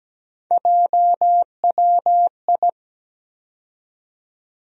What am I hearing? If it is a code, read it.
Morse code, heard as a keyed tone, about 17 wpm, 700 Hz. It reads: JWI
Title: USA Call Sign Postfixes - 3 Letters 17wpm